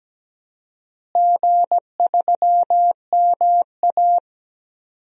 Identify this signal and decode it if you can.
Morse code, heard as a keyed tone, about 17 wpm, 700 Hz. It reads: G3MA
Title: DXCC-100 Call Signs - 4 Characters - Repeated 2x - 17wpm